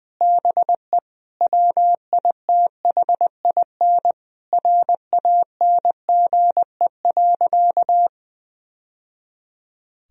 Morse code, heard as a keyed tone, about 21 wpm, 700 Hz. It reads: BE WITHIN RANGE.
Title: Sentences from Top 1000 Words - Review Part 4 - 20wpm